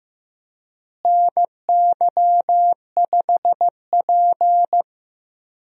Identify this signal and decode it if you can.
Morse code, heard as a keyed tone, about 15 wpm, 700 Hz. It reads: NY5P
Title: USA Call Signs - 4 Characters - Repeated 2x - 15wpm